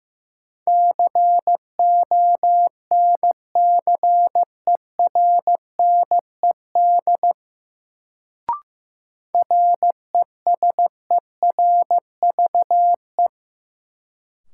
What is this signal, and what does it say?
Morse code, heard as a keyed tone, about 15 wpm, 700 Hz. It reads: CONCERNED RESERVE
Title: Sentences from Top 1000 Words - Review Part 6 - New Word Drill - 15wpm